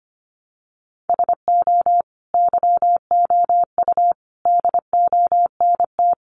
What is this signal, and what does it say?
Morse code, heard as a keyed tone, about 25 wpm, 700 Hz. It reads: SO YOU DONT
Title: Sentences from Top 1000 Words 25wpm